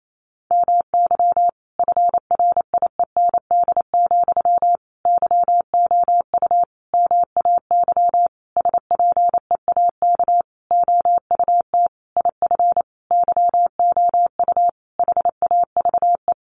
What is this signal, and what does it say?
Morse code, heard as a keyed tone, about 28 wpm, 700 Hz. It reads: MY FRIEND, YOU MAY SPEAK OUT IF YOU HAVE
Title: Sentences from Top 1300 Words 28wpm